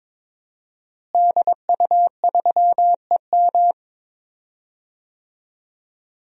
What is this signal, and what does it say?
Morse code, heard as a keyed tone, about 22 wpm, 700 Hz. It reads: DU3EM
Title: DXCC-100 Call Signs - 5 Characters - Repeated 3x - 22wpm